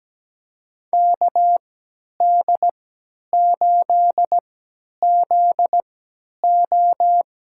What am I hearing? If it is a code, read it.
Morse code, heard as a keyed tone, about 17 wpm, spaced as for 10 wpm, 700 Hz. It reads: KD8ZO
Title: USA Call Signs - 3x Character Spacing - 17wpm